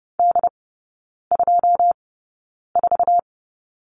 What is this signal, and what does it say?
Morse code, heard as a keyed tone, about 30 wpm, spaced as for 9 wpm, 700 Hz. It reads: D24
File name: Lesson 16e - Letter c - Three Characters 30wpm